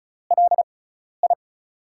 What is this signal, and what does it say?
Morse code, heard as a keyed tone, about 36 wpm, spaced as for 12 wpm, 700 Hz. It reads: LI
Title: USA Call Signs - 4 Characters - 6x Character Spacing - 35wpm